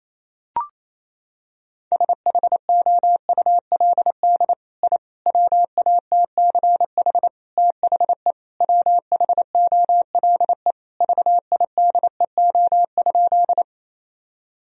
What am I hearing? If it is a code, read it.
Morse code, heard as a keyed tone, about 27 wpm, 700 Hz. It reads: SHOULD I WATCH THE WHOLE VIDEO?
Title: Sentences from Top 1000 Words - Review Part 4 - 28wpm